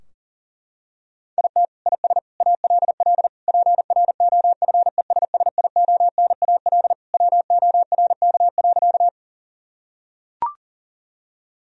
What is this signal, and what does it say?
Morse code, heard as a keyed tone, about 40 wpm, 700 Hz. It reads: IT IS ALL PROFESSIONAL WORK.
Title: Sentences from Top 1000 Words - Review Part 4 - 40wpm